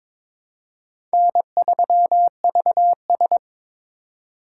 Morse code, heard as a keyed tone, about 22 wpm, 700 Hz. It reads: N3VS
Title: USA Call Signs - 4 Characters - 3x Character Spacing - 22wpm